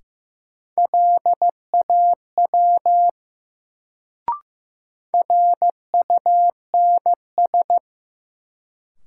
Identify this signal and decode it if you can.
Morse code, heard as a keyed tone, about 15 wpm, 700 Hz. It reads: LAW RUNS
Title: Sentences from Top 600 Words - New Word Drill - 15wpm